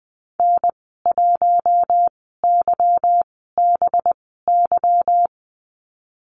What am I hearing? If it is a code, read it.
Morse code, heard as a keyed tone, about 20 wpm, spaced as for 15 wpm, 700 Hz. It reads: N1YBY
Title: 193 Lesson 31g - Letter v - Call Signs 20wpm